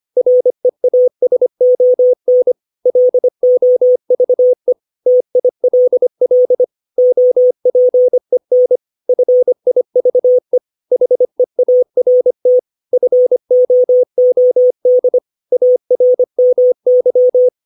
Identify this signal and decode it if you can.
Morse code, heard as a keyed tone, about 25 wpm, 500 Hz. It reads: REASON LOVE TILL OPEN FIVE HEART FOOD ARMY
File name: 632 words / 632 Words 25WPM